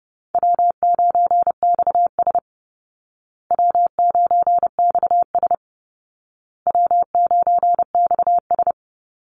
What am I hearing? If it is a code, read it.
Morse code, heard as a keyed tone, about 30 wpm, 700 Hz. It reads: W9XS W9XS W9XS
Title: Call Signs Repeated 2x - 30wpm